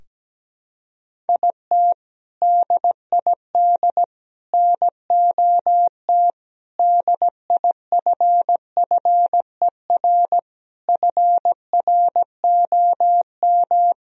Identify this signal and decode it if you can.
Morse code, heard as a keyed tone, about 17 wpm, 700 Hz. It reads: IT DID NOT DIFFER FROM
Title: Sentences from Top 1000 Words - Review Part 1 - 17wpm